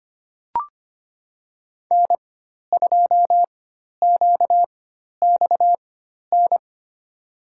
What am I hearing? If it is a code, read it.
Morse code, heard as a keyed tone, about 25 wpm, spaced as for 12 wpm, 700 Hz. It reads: N2QXN
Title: USA Call Signs - 5 Characters - 4x Character Spacing - 25wpm